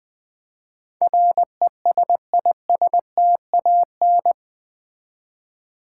Morse code, heard as a keyed tone, about 20 wpm, 700 Hz. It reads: RESISTAN
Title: Build A Word - 4 Letters At A Time - 10-Letter Words - 20wpm